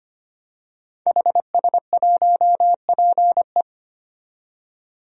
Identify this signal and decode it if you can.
Morse code, heard as a keyed tone, about 25 wpm, 700 Hz. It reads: HS1PE
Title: DXCC-100 Call Signs - 5 Characters - 25wpm